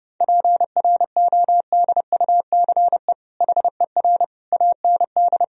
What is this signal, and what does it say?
Morse code, heard as a keyed tone, about 30 wpm, 700 Hz. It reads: PRODUCE HER AND
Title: Sentences from Top 1000 Words - Review Part 3 - 30wpm